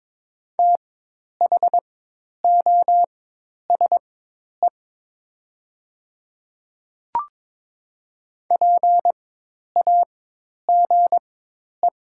Morse code, heard as a keyed tone, about 22 wpm, spaced as for 10 wpm, 700 Hz. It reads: THOSE PAGE
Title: 4x Character Spacing - Top 300 Words - 22wpm